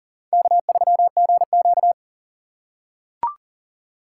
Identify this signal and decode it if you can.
Morse code, heard as a keyed tone, about 40 wpm, 700 Hz. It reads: K3GQ
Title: USA Call Signs - 6x Character Spacing - 40wpm